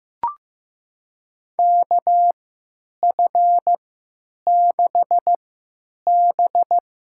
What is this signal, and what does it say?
Morse code, heard as a keyed tone, about 15 wpm, spaced as for 9 wpm, 700 Hz. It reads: KF6B
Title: USA Call Signs - 3x Character Spacing - 15wpm